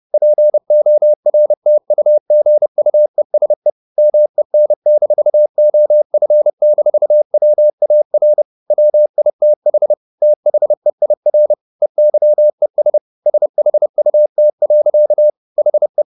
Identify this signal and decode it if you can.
Morse code, heard as a keyed tone, about 30 wpm, 600 Hz. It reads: PORTUGUESE MEN-OF-WAR WITH THEIR EYES SHUT. HE
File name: the old man and the sea / the old man and the sea 1 30wpm